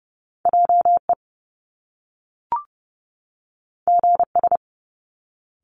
Morse code, heard as a keyed tone, about 30 wpm, 700 Hz. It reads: JE GS
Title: Top Two Letter Combinations in Words 30wpm